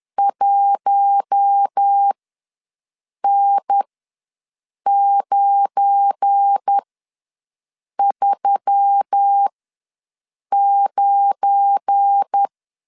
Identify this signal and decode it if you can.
Morse code, heard as a keyed tone, about 10 wpm, spaced as for 6 wpm, 800 Hz. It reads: 1N939
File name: LEZIONI / 7 GF9Z / 7 GF9 ws 10wpm 4034